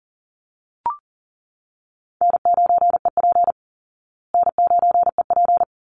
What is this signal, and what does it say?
Morse code, heard as a keyed tone, about 37 wpm, 700 Hz. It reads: N9EP N9EP
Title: Easy Call Signs Repeated 2x - 40wpm